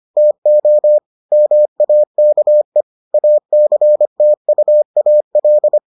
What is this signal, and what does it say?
Morse code, heard as a keyed tone, about 25 wpm, 600 Hz. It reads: TO MAKE ACTUAL